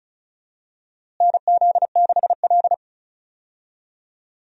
Morse code, heard as a keyed tone, about 34 wpm, 700 Hz. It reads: NZ6L
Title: USA Call Signs - 4 Characters - Repeated 2x - 35wpm